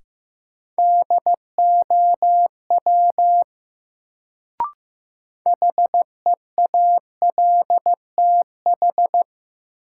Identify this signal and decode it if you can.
Morse code, heard as a keyed tone, about 16 wpm, 700 Hz. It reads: DOW HEALTH
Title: Sets of 5 Words - New Word Drill - 15wpm